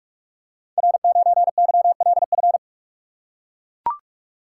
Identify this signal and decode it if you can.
Morse code, heard as a keyed tone, about 44 wpm, 700 Hz. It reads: R9YLF
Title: Code Groups - 5 Letter:Numbers - 45wpm